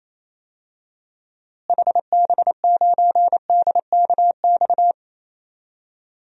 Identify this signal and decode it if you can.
Morse code, heard as a keyed tone, about 27 wpm, 700 Hz. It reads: HB9DKX
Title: DXCC-100 Call Signs - 6 Characters - Repeated 3x - 28wpm